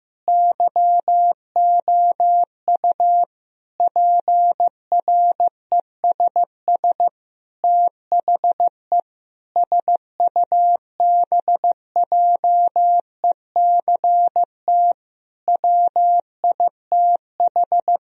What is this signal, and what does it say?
Morse code, heard as a keyed tone, about 16 wpm, 700 Hz. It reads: YOU PRESS THE SUBJECT WITH
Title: Sentences from Top 1000 Words - Review Part 1 - 15wpm